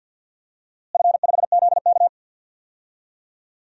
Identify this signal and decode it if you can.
Morse code, heard as a keyed tone, about 50 wpm, 700 Hz. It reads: F5ZK
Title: DXCC-100 Call Signs - 4 Characters - Repeated 2x - 50wpm